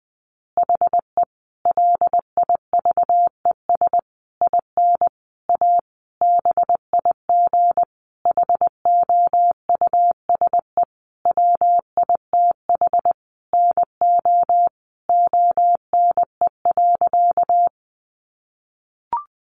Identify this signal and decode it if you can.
Morse code, heard as a keyed tone, about 20 wpm, 700 Hz. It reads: HE LIVES IN A BIG HOUSE WITH NO ONE.
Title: Additional Easy Phrases 20wpm